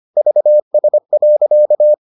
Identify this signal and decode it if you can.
Morse code, heard as a keyed tone, about 25 wpm, 600 Hz. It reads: VS.